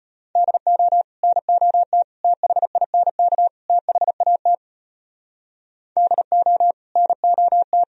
Morse code, heard as a keyed tone, about 32 wpm, 700 Hz. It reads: DO NOT THINK THAT DO NOT